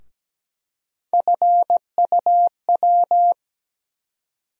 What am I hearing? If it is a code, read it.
Morse code, heard as a keyed tone, about 17 wpm, 700 Hz. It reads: FUW
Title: USA Call Sign Postfixes - 3 Letters 17wpm